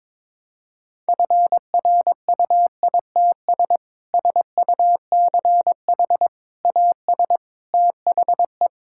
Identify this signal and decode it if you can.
Morse code, heard as a keyed tone, about 22 wpm, 700 Hz. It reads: FRUITS SUCH AS THE
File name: practice / 22wpm / Sentences with 3-Word Phrases - 22wpm